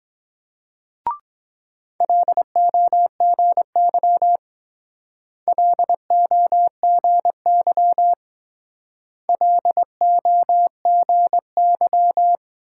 Morse code, heard as a keyed tone, about 22 wpm, 700 Hz. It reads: LOGY LOGY LOGY